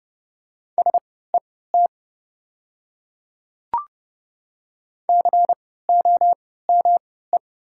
Morse code, heard as a keyed tone, about 30 wpm, spaced as for 17 wpm, 700 Hz. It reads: SET COME
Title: 3x Character Spacing - Top 300 Words - 30wpm